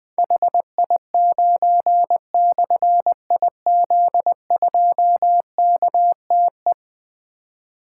Morse code, heard as a keyed tone, about 20 wpm, 700 Hz. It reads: HI9/IZ2KTE
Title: Difficult Call Signs 20wpm